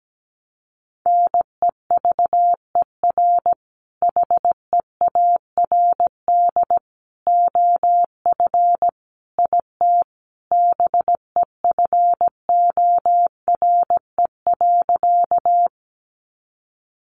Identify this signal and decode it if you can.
Morse code, heard as a keyed tone, about 17 wpm, 700 Hz. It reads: NEVER HEARD OF IT BEFORE.
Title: Sentences from Top 400 Words 17wpm